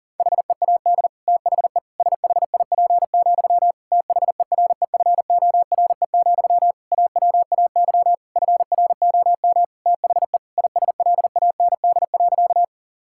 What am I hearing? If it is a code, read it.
Morse code, heard as a keyed tone, about 40 wpm, 700 Hz. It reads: HEAD THE SHIP, THEREFORE, AWAY FROM THE ISLAND.
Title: Sentences from Top 1000 Words - Review Part 5 - 40wpm